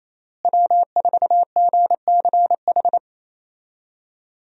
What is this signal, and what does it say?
Morse code, heard as a keyed tone, about 28 wpm, 700 Hz. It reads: W4GCH
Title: USA Call Signs - 5 Characters - Repeated 3x - 28wpm